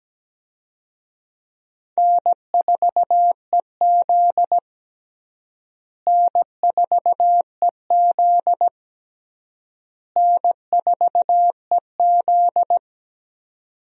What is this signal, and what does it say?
Morse code, heard as a keyed tone, about 17 wpm, 700 Hz. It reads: N4EZ N4EZ N4EZ
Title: USA Call Signs - 4 Characters - Repeated 3x - 17wpm